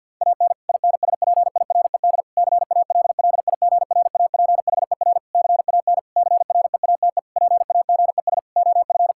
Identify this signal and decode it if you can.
Morse code, heard as a keyed tone, about 50 wpm, 700 Hz. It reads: AN INSPIRED CALLIGRAPHER CAN CREATE PAGES OF
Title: Pangram Challenge v2 - 50wpm